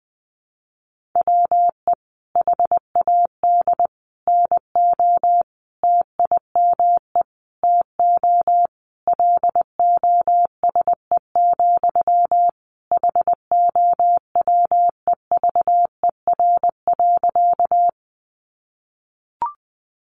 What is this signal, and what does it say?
Morse code, heard as a keyed tone, about 19 wpm, 700 Hz. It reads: WE HAD NO TIME TO LOSE, HOWEVER.
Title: Sentences from Top 1100 Words 20wpm